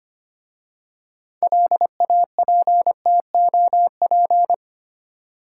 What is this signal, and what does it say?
Morse code, heard as a keyed tone, about 25 wpm, 700 Hz. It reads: LAPTOP
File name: practice / 25wpm / Easy Words with Hints - 25wpm